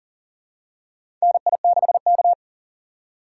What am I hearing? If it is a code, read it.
Morse code, heard as a keyed tone, about 39 wpm, 700 Hz. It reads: NI6K
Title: USA Call Signs - 4x Character Spacing - 40wpm